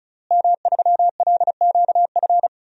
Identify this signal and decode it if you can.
Morse code, heard as a keyed tone, about 35 wpm, 700 Hz. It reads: M3LQF
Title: DXCC-100 Call Signs - 5 Characters - Repeated 2x - 35wpm